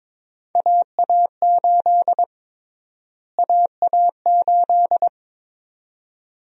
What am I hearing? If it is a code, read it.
Morse code, heard as a keyed tone, about 22 wpm, 700 Hz. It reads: AA8 AA8
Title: USA Call Sign Prefixes - Repeated 2x - 22wpm